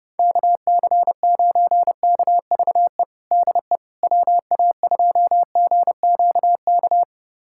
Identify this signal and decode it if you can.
Morse code, heard as a keyed tone, about 31 wpm, 700 Hz. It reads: KC9KVE DE WA2GQK
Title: USA Callsign DE Callsign - 6 Characters - 30wpm